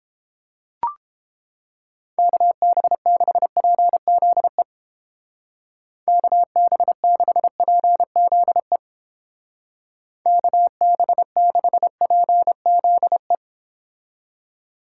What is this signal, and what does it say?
Morse code, heard as a keyed tone, about 29 wpm, 700 Hz. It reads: KB6PZE KB6PZE KB6PZE